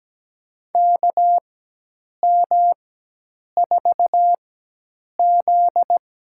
Morse code, heard as a keyed tone, about 17 wpm, spaced as for 8 wpm, 700 Hz. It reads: KM4Z
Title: USA Call Signs - 4x Character Spacing - 17wpm